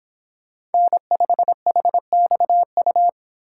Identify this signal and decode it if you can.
Morse code, heard as a keyed tone, about 30 wpm, 700 Hz. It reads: N5HXU